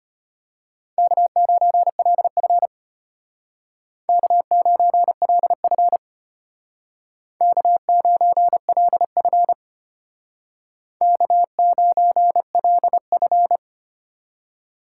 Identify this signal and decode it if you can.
Morse code, heard as a keyed tone, about 32 wpm, 700 Hz. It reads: K9LF K9LF K9LF K9LF